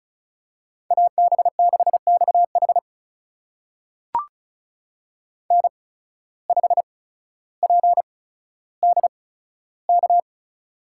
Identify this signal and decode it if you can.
Morse code, heard as a keyed tone, about 36 wpm, spaced as for 15 wpm, 700 Hz. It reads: AB6XH N5PDK
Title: USA Call Signs - 5 Characters - 8x Character Spacing - 35wpm